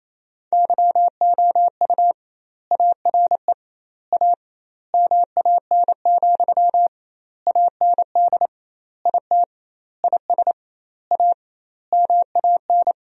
Morse code, heard as a keyed tone, about 28 wpm, 700 Hz. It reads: YOU ARE A MAN, AND IT IS A MAN
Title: 2x Word-Spacing - Sentences from Top 200 Words - 28wpm